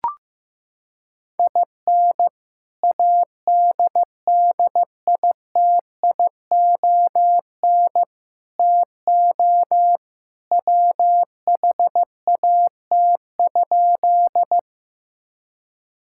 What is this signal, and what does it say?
Morse code, heard as a keyed tone, about 16 wpm, 700 Hz. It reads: IN ADDITION TO WHAT?
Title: Sentences from Top 1000 Words - Review Part 6 - 15wpm